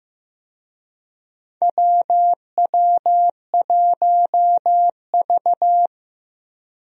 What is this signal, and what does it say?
Morse code, heard as a keyed tone, about 15 wpm, 700 Hz. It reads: WW1V
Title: USA Call Signs - 4 Characters - Repeated 2x - 15wpm